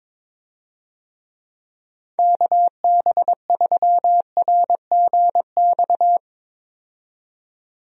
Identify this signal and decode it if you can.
Morse code, heard as a keyed tone, about 22 wpm, 700 Hz. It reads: KB3RGX
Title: USA Call Signs - 6 Characters - 22wpm